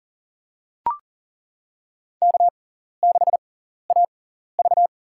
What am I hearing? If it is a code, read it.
Morse code, heard as a keyed tone, about 40 wpm, spaced as for 14 wpm, 700 Hz. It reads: K6AV
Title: USA Call Signs - 4 Characters - 6x Character Spacing - 40wpm